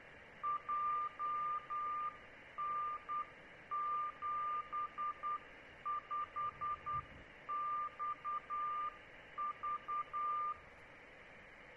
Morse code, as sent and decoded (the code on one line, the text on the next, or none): .--- -. --... ..... -..- ...-
JN75XV